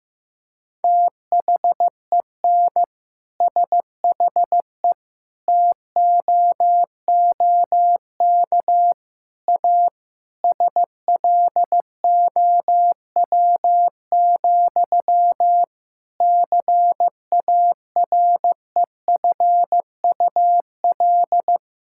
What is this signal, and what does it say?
Morse code, heard as a keyed tone, about 15 wpm, 700 Hz. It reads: THEN SHE TOOK A SLOW, CAREFUL
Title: Sentences from Top 1600 Words 15wpm